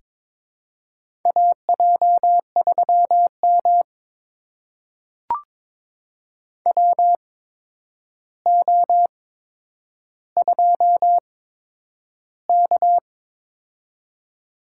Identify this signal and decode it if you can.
Morse code, heard as a keyed tone, about 22 wpm, spaced as for 9 wpm, 700 Hz. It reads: AJ3M WO2K
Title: USA Call Signs - 4 Characters - 8x Character Spacing - 22wpm